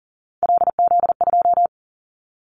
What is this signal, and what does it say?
Morse code, heard as a keyed tone, about 37 wpm, 700 Hz. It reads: LZ2
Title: Callsign Prefixes for Top 25 DXCC - 40wpm